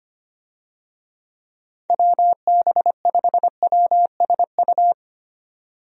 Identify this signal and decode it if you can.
Morse code, heard as a keyed tone, about 25 wpm, 700 Hz. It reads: WB5WSU
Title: USA Call Signs - 6 Characters - Repeated 2x - 25wpm